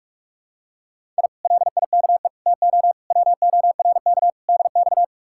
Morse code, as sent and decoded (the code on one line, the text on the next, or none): .. / .-.. .. -.- . / - --- / .-- --- .-. -.- / -.. -..-
I LIKE TO WORK DX